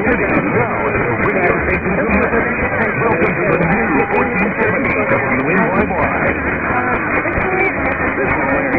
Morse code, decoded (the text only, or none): none